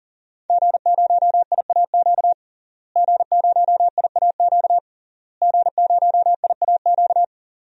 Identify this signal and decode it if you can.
Morse code, heard as a keyed tone, about 40 wpm, 700 Hz. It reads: G0IAQ G0IAQ G0IAQ
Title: DXCC-100 Call Signs - 5 Characters - Repeated 3x - 40wpm